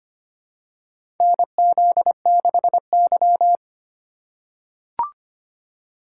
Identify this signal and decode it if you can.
Morse code, heard as a keyed tone, about 25 wpm, 700 Hz. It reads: NZ6Y
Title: USA Call Signs - 4 Characters - 3x Character Spacing - 25wpm